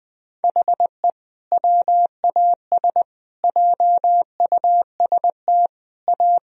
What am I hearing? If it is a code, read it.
Morse code, heard as a keyed tone, about 20 wpm, 700 Hz. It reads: HE WAS JUST A
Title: Sentences from Top 1000 Words - Review Part 1 - 20wpm